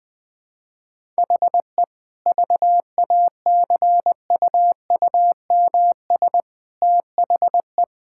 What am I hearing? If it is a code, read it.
Morse code, heard as a keyed tone, about 20 wpm, 700 Hz. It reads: HE VACUUMS THE
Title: Easy Sentences with Hints - 20wpm